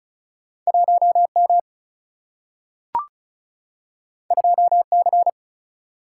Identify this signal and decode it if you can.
Morse code, heard as a keyed tone, about 36 wpm, 700 Hz. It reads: 1 M 2 C
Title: Single-Letter Single-Number 35wpm